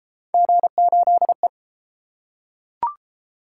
Morse code, heard as a keyed tone, about 29 wpm, 700 Hz. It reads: G8E